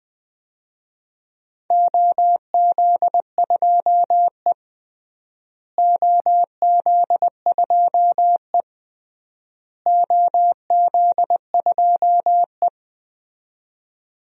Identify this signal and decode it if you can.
Morse code, heard as a keyed tone, about 21 wpm, 700 Hz. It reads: OZ2E OZ2E OZ2E
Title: DXCC-100 Call Signs - 4 Characters - Repeated 3x - 20wpm